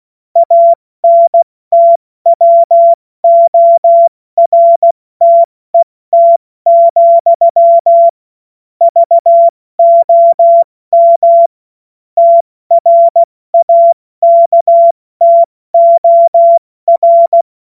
Morse code, heard as a keyed tone, about 16 wpm, 680 Hz. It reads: ANTWORTET, VOM TRAKTOR
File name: Morsekurs / Witze / Witz 12. Oktober 2021, 70 F80